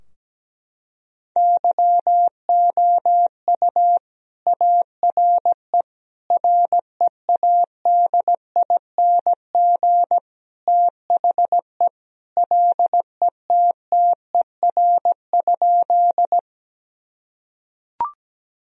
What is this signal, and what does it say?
Morse code, heard as a keyed tone, about 17 wpm, 700 Hz. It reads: YOU ARE READING THE LETTER?
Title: Sentences from Top 1000 Words - Review Part 3 - 17wpm